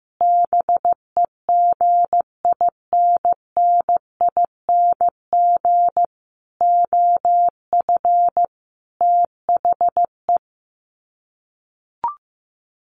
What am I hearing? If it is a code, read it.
Morse code, heard as a keyed tone, about 15 wpm, 700 Hz. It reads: BEGINNING OF THE
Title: Sets of 5 Words Encore2 15wpm